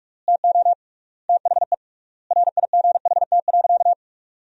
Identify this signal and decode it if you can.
Morse code, heard as a keyed tone, about 46 wpm, 700 Hz. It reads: TO THE RIGHT.
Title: 3x Word-Spacing - Sentences from Top 300 Words - 45wpm